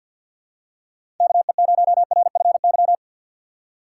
Morse code, heard as a keyed tone, about 50 wpm, 700 Hz. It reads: KE0RFY
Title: USA Call Signs - 6 Characters - 4x Character Spacing - 50wpm